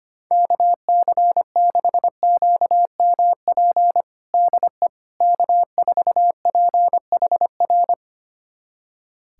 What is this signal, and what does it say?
Morse code, heard as a keyed tone, about 25 wpm, 700 Hz. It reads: KC6QMP DE K4PHR
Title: USA Callsign DE Callsign 25wpm